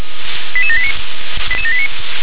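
Five-tone selective call (ZVEI), heard as quick runs of five tones, tones of about 70 ms. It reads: 88789 88789
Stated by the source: ZVEI five-tone selective call received on SDR.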